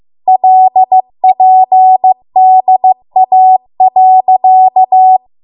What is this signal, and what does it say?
Morse code, heard as a keyed tone, about 15 wpm, 750 Hz. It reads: LPDA.